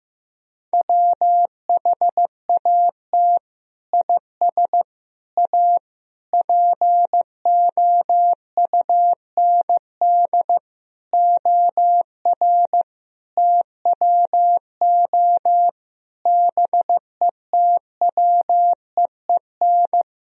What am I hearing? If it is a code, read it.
Morse code, heard as a keyed tone, about 15 wpm, 700 Hz. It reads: WHAT IS A POUND OR TWO BETWEEN
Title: Sentences from Top 1000 Words - Review Part 2 - 15wpm